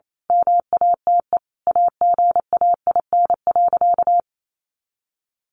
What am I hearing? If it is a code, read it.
Morse code, heard as a keyed tone, about 28 wpm, 700 Hz. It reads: MATE AGAIN.
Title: A Sound Of Thunder 28wpm